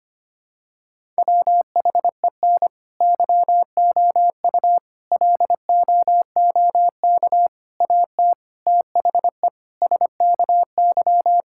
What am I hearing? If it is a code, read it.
Morse code, heard as a keyed tone, about 25 wpm, 700 Hz. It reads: WHEN YOU LOOK AT THE SKY
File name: practice / 25wpm / Sentences with 3-Word Phrases - 25wpm